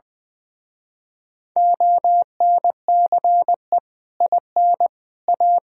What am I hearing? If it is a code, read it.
Morse code, heard as a keyed tone, about 20 wpm, 700 Hz. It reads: ONCE IN A
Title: Speed-Racing - Sets of 4 Words 20wpm